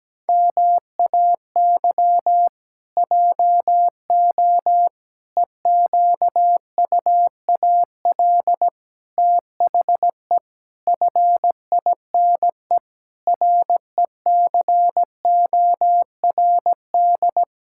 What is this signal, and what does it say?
Morse code, heard as a keyed tone, about 17 wpm, 700 Hz. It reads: MAY JO EQUAL THE FINE RECORD
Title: Pangram Challenge v2 - 17wpm